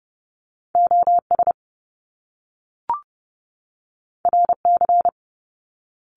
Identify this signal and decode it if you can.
Morse code, heard as a keyed tone, about 30 wpm, 700 Hz. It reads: OS RC
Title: Top Two Letter Combinations in Words 30wpm